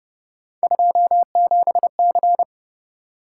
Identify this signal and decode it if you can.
Morse code, heard as a keyed tone, about 29 wpm, 700 Hz. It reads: 27C